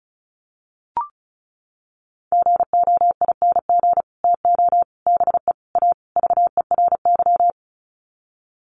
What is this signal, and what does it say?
Morse code, heard as a keyed tone, about 34 wpm, 700 Hz. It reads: GOING TO BE A VERY
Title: Sets of 5 Words Encore 35wpm